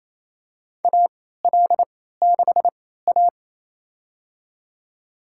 Morse code, heard as a keyed tone, about 28 wpm, spaced as for 16 wpm, 700 Hz. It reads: AL6A
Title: USA Call Signs - 4 Characters - 3x Character Spacing - 28wpm